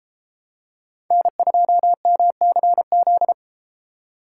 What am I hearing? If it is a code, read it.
Morse code, heard as a keyed tone, about 29 wpm, 700 Hz. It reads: N2MCZ